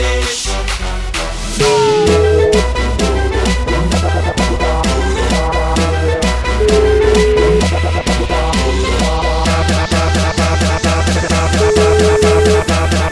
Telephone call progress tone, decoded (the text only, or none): ringback tone